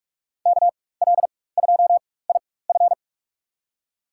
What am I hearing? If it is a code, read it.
Morse code, heard as a keyed tone, about 47 wpm, spaced as for 21 wpm, 700 Hz. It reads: KL2IF
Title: USA Call Signs - 5 Characters - 4x Character Spacing - 45wpm